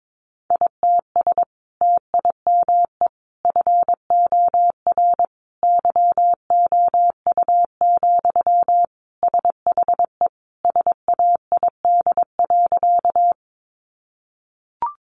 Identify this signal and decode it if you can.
Morse code, heard as a keyed tone, about 22 wpm, 700 Hz. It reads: ITS TIME FOR YOU, SHE SAID.
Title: Sentences from Top 600 Words 22wpm